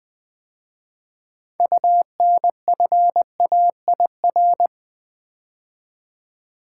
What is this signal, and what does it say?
Morse code, heard as a keyed tone, about 20 wpm, 700 Hz. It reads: UNFAIR
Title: 6 Letter Words - First 3 Letters Given - 20wpm